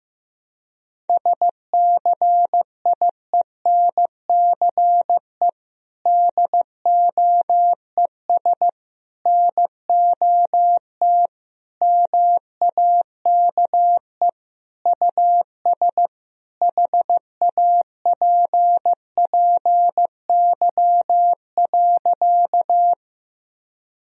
Morse code, heard as a keyed tone, about 15 wpm, 700 Hz. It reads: SCIENCE DOES NOT MAKE US HAPPY.
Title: Sentences from Top 1000 Words - Review Part 2 - 15wpm